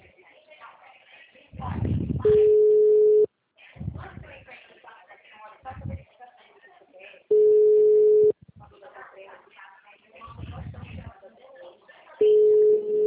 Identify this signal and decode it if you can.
Ringback tone (Europe, 425 Hz), with speech.